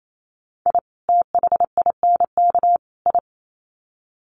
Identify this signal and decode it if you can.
Morse code, heard as a keyed tone, about 28 wpm, 700 Hz. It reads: I THINK I
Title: Sets of 3 Words 28wpm